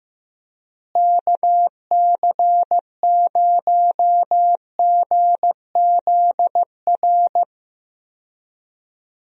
Morse code, heard as a keyed tone, about 15 wpm, 700 Hz. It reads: KC0GZR
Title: USA Call Signs - 6 Characters - Repeated 2x - 15wpm